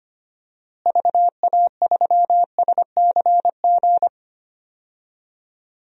Morse code, heard as a keyed tone, about 25 wpm, 700 Hz. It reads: VA3SCG
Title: DXCC-100 Call Signs - 6 Characters - Repeated 2x - 25wpm